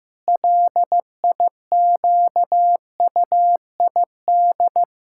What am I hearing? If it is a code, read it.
Morse code, heard as a keyed tone, about 16 wpm, 700 Hz. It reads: LIQUID
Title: Pangram Challenge v2 - 15wpm